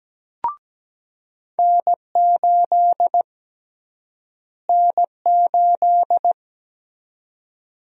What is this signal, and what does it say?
Morse code, heard as a keyed tone, about 17 wpm, 700 Hz. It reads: N8 N8
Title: USA Call Sign Prefixes - Repeated 2x - 17wpm